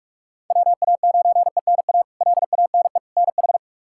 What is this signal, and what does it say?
Morse code, heard as a keyed tone, about 42 wpm, 700 Hz. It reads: WA9ENA LANE NH
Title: K1USN SST QSO Practice - 45wpm